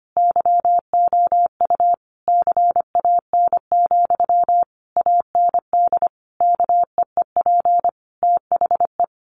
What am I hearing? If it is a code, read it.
Morse code, heard as a keyed tone, about 25 wpm, 700 Hz. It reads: YOU CAN, AND KEEP THE
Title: Sentences from Top 1300 Words 25wpm